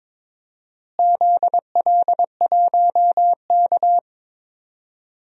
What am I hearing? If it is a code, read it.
Morse code, heard as a keyed tone, about 22 wpm, 700 Hz. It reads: ZL1K
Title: DXCC-100 Call Signs - 4 Characters - Repeated 2x - 22wpm